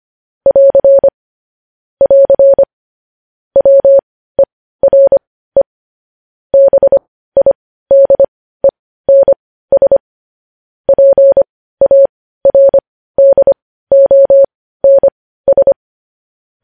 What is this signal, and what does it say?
Morse code, heard as a keyed tone, about 25 wpm, spaced as for 15 wpm, 550 Hz. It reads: <AR> <AR> WERE BIDENS PARDONS
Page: news headlines in morse code at 15 wpm